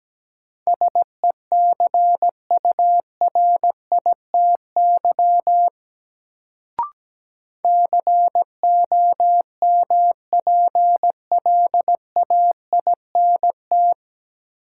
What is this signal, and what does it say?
Morse code, heard as a keyed tone, about 17 wpm, 700 Hz. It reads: SECURITY COMPLAINT
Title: Sentences from Top 1400 Words - New Word Drill Alt - 17wpm